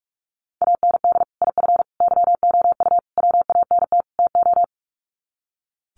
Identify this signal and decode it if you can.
Morse code, heard as a keyed tone, about 44 wpm, 700 Hz. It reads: AND IF YOU WANT TO
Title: Sets of 5 Words Encore 45wpm